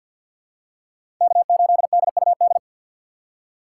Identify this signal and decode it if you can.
Morse code, heard as a keyed tone, about 47 wpm, 700 Hz. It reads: K8DUD
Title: USA Call Signs - 5 Characters - 2x Character Spacing - 50wpm